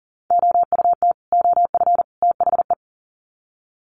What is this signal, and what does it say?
Morse code, heard as a keyed tone, about 37 wpm, 700 Hz. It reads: OUT OF THE
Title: Sets of 3 Words 40wpm